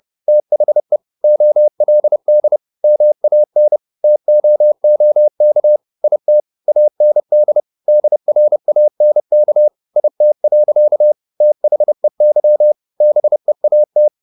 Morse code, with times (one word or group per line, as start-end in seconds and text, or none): 0.3-1.0 THE
1.2-2.6 OLD
2.8-3.8 MAN
4.0-5.8 TOOK
6.0-6.4 IT
6.7-7.6 AND
7.9-9.7 DRANK
10.0-11.1 IT.
11.4-12.7 THEY
13.0-14.1 BEAT